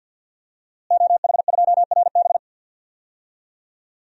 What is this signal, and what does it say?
Morse code, heard as a keyed tone, about 50 wpm, 700 Hz. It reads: OH2RB
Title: DXCC-100 Call Signs - 5 Characters - Repeated 2x - 50wpm